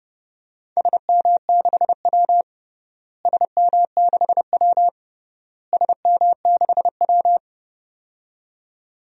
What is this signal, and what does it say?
Morse code, heard as a keyed tone, about 30 wpm, 700 Hz. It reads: SM6W SM6W SM6W
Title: DXCC-100 Call Signs - 4 Characters - Repeated 3x - 30wpm